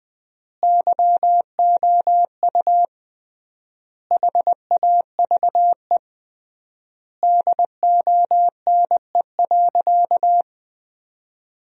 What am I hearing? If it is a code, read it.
Morse code, heard as a keyed tone, about 20 wpm, 700 Hz. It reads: YOU HAVE DONE.
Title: 3x Word-Spacing - Sentences from Top 500 Words - 20wpm